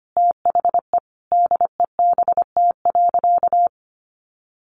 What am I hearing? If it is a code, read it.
Morse code, heard as a keyed tone, about 25 wpm, 700 Hz. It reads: THE DEBT.
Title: Sentences from Top 1300 Words 25wpm